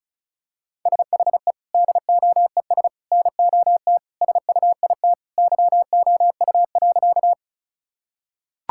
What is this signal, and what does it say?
Morse code, heard as a keyed tone, about 36 wpm, 700 Hz. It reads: SHE DOES NOT SUIT YOU.
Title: Sentences from Top 1600 Words 35wpm